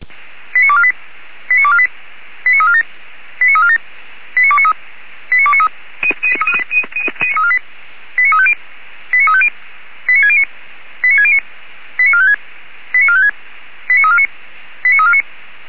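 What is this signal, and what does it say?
Five-tone selective call (ZVEI), heard as quick runs of five tones, tones of about 70 ms. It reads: 89248 89248 89347 89347 89293 89293 89350 89350 89609 89609 89456 89456 89349 89349
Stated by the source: ZVEI five-tone selective call received via SDR.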